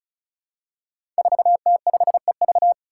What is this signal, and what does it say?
Morse code, heard as a keyed tone, about 34 wpm, 700 Hz. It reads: 4T5EV
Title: Five Challenging Letter-Number Combinations 35wpm